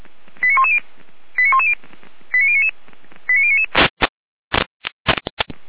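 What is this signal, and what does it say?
Five-tone selective call (ZVEI), heard as quick runs of five tones, tones of about 70 ms. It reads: 89110 89110 89099 89099
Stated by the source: Five-tone ZVEI call 89080, received by SDR.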